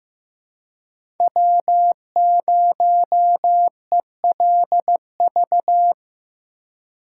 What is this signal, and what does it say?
Morse code, heard as a keyed tone, about 15 wpm, 700 Hz. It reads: W0ELV
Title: Code Groups - 5 Letter:Numbers - 15wpm